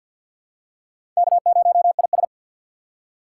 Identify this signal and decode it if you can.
Morse code, heard as a keyed tone, about 51 wpm, 700 Hz. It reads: K0IS
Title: USA Call Signs - 4 Characters - 3x Character Spacing - 50wpm